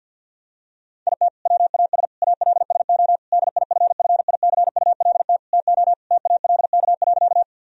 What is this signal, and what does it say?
Morse code, heard as a keyed tone, about 50 wpm, 700 Hz. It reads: IT WAS ALSO DIFFICULT TO TALK.
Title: Sentences from Top 1000 Words - Review Part 1 - 50wpm